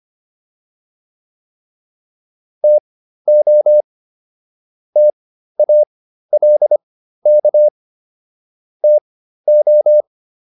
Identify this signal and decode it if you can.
Morse code, heard as a keyed tone, about 25 wpm, spaced as for 13 wpm, 600 Hz. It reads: TO TALK TO